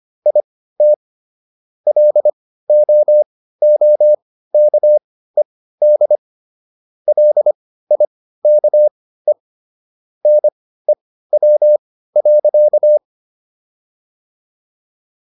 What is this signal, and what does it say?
Morse code, heard as a keyed tone, about 25 wpm, spaced as for 15 wpm, 600 Hz. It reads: IT LOOKED LIKE NEW.